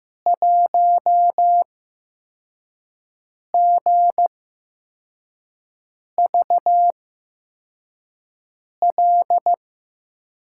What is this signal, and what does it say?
Morse code, heard as a keyed tone, about 15 wpm, spaced as for 4 wpm, 700 Hz. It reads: 1GVL
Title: USA Call Signs - 5 Characters - 8x Character Spacing - 15wpm